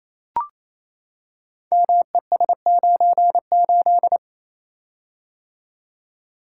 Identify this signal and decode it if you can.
Morse code, heard as a keyed tone, about 27 wpm, 700 Hz. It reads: MES98
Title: Code Groups - 5 Letter:Numbers - 28wpm